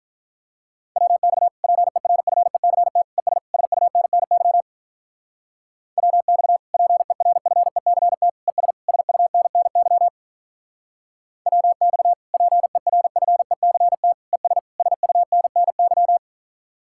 Morse code, heard as a keyed tone, about 46 wpm, 700 Hz. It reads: WX PERFECT ES SUNNY WX PERFECT ES SUNNY WX PERFECT ES SUNNY